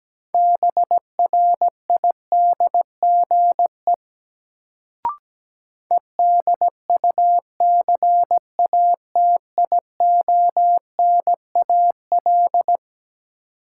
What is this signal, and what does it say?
Morse code, heard as a keyed tone, about 17 wpm, 700 Hz. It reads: BRIDGE EDUCATIONAL
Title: Sentences from Top 1200 Words - New Word Drill - 17wpm